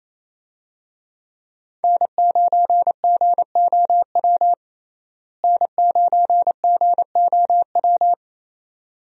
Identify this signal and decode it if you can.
Morse code, heard as a keyed tone, about 28 wpm, 700 Hz. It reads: N9GOW N9GOW
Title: USA Call Signs - 5 Characters - Repeated 2x - 28wpm